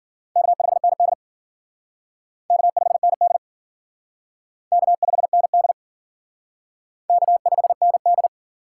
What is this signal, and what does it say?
Morse code, heard as a keyed tone, about 50 wpm, 700 Hz. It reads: K5ND K5ND K5ND K5ND